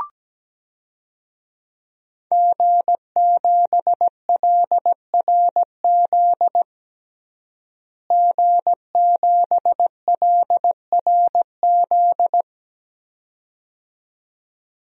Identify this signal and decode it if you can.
Morse code, heard as a keyed tone, about 17 wpm, 700 Hz. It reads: G7LRZ G7LRZ
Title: DXCC-100 Call Signs - 5 Characters - Repeated 2x - 17wpm